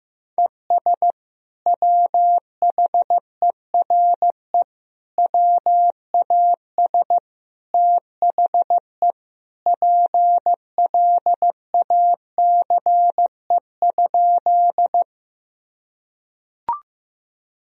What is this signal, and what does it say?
Morse code, heard as a keyed tone, about 15 wpm, 700 Hz. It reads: ES WHERE WAS THE PLACE?
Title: Sentences from Top 200 Words and Abbreviations 15wpm